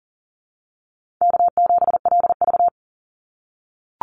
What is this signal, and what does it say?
Morse code, heard as a keyed tone, about 40 wpm, 700 Hz. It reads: K7LV
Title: Easy Call Signs - 40wpm